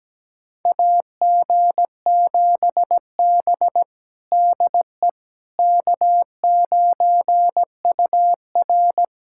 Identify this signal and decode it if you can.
Morse code, heard as a keyed tone, about 17 wpm, 700 Hz. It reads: AG7B DE K9UR
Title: USA Callsign DE Callsign - 4 Characters - 17wpm